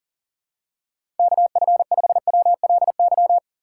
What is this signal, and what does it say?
Morse code, heard as a keyed tone, about 42 wpm, 700 Hz. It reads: KF5WLY
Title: USA Call Signs - 6x Character Spacing - 40wpm